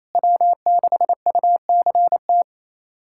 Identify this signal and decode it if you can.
Morse code, heard as a keyed tone, about 28 wpm, 700 Hz. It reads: W6UCT
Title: USA Callsign DE Callsign - 5 Characters - 28wpm